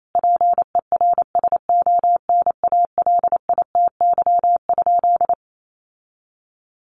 Morse code, heard as a keyed tone, about 28 wpm, 700 Hz. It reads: PERSONALITY?
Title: Sentences from Top 1400 Words 28wpm